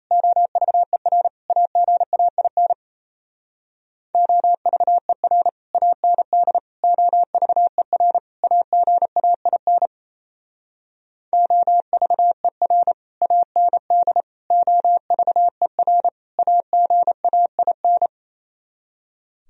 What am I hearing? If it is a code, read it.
Morse code, heard as a keyed tone, about 34 wpm, 690 Hz. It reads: OVER AGAIN OVER AND OVER AGAIN OVER AND OVER AGAIN
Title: Speed-Racing - Sets of 4 Words 28wpm